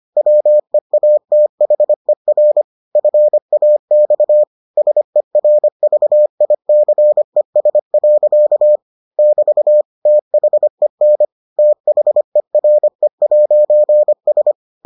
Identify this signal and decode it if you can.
Morse code, heard as a keyed tone, about 25 wpm, 600 Hz. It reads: WEATHER FAX SERVICES. <BT> THEN THERE'S